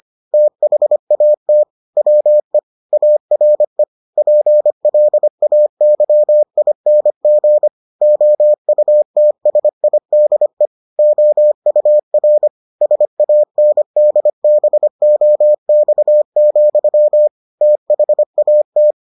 Morse code, heard as a keyed tone, about 25 wpm, 600 Hz. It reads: THAT WE ARE PLAYING OUTSIDE OUR SANDBOX, THAT